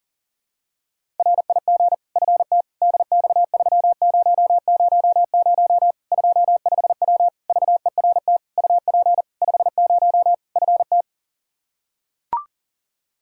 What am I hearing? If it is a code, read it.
Morse code, heard as a keyed tone, about 40 wpm, 700 Hz. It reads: RIG FT DX3000 25W VERT UP 50 FT
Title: Practice QSO - Exchange 2 - Info - 40wpm